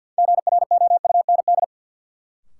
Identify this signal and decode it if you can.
Morse code, heard as a keyed tone, about 46 wpm, 700 Hz. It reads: GROUND
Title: Sets of 3 Words - New Word Drill - 50wpm